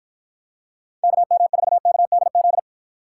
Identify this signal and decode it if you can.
Morse code, heard as a keyed tone, about 53 wpm, 700 Hz. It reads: KM4KDZ